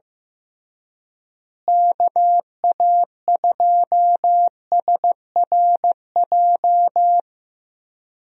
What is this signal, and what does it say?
Morse code, heard as a keyed tone, about 15 wpm, 700 Hz. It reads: KA2SRJ
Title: USA Call Signs - 6 Characters - Repeated 3x - 15wpm